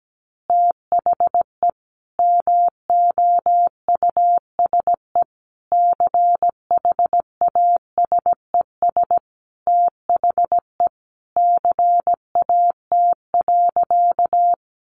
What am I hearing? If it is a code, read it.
Morse code, heard as a keyed tone, about 17 wpm, 700 Hz. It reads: THE MOUSE CHASES THE CAT.
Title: Easy Phrases 17wpm